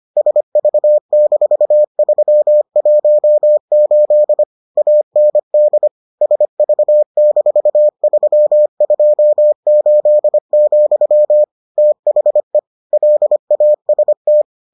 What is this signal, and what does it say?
Morse code, heard as a keyed tone, about 25 wpm, 600 Hz. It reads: SV-318 AND SV-328, THE LAST